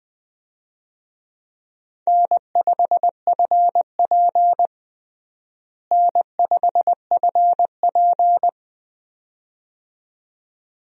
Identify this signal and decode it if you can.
Morse code, heard as a keyed tone, about 20 wpm, 700 Hz. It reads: N5FP N5FP
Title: USA Call Signs - 4 Characters - Repeated 2x - 20wpm